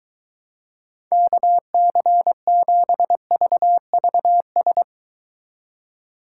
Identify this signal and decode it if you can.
Morse code, heard as a keyed tone, about 23 wpm, 700 Hz. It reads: KC7VVS